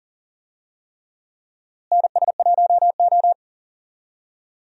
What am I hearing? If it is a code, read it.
Morse code, heard as a keyed tone, about 40 wpm, 700 Hz. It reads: NS1O
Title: USA Call Signs - 4 Characters - Repeated 3x - 40wpm